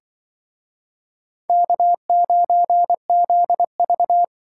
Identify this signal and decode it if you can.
Morse code, heard as a keyed tone, about 24 wpm, 700 Hz. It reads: K9ZV
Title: Speed Racing - USA 1x2 Call Signs 20wpm